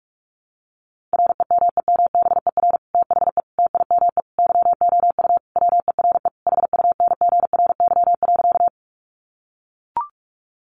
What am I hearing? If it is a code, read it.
Morse code, heard as a keyed tone, about 44 wpm, 700 Hz. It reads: REMEMBER THE TIME YOU WERE HUNGRY.
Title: Sentences from Top 1400 Words 45wpm